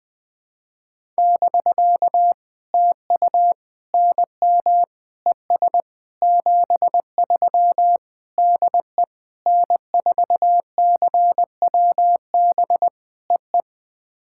Morse code, heard as a keyed tone, about 20 wpm, 700 Hz. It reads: <BK> TU NM ES 73 DE N4CWB EE
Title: POTA QSO Practice - 20wpm